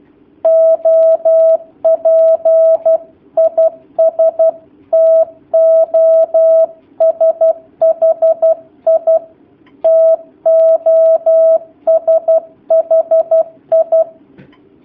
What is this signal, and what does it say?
Morse code, heard as a keyed tone, about 12 wpm, 650 Hz. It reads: OP IS TOSHI TOSHI
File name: jh8cbh / 2019cw / sp11- w